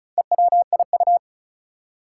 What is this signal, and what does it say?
Morse code, heard as a keyed tone, about 36 wpm, 700 Hz. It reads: EWIU
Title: Four Challenging Letter-Number Combinations 35wpm